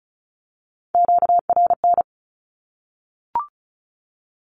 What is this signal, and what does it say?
Morse code, heard as a keyed tone, about 34 wpm, 700 Hz. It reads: QRN
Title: Q-Codes 35wpm